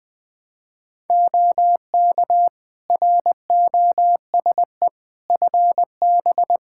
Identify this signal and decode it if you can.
Morse code, heard as a keyed tone, about 20 wpm, 700 Hz. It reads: OK ROSE FB
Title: Practice QSO - Exchange 2 - Info - 20wpm